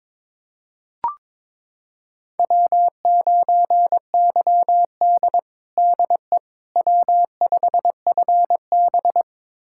Morse code, heard as a keyed tone, about 22 wpm, 700 Hz. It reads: W9YD DE W5FB
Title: USA Callsign DE Callsign - 4 Characters - 22wpm